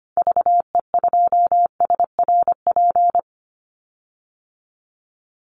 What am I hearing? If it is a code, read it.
Morse code, heard as a keyed tone, about 25 wpm, 700 Hz. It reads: VE2SRP
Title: Call Signs - 25wpm